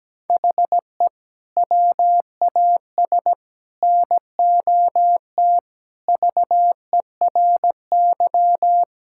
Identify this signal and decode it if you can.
Morse code, heard as a keyed tone, about 17 wpm, 700 Hz. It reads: HE WAS NOT VERY
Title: Sentences from Top 1000 Words - Review Part 5 - 17wpm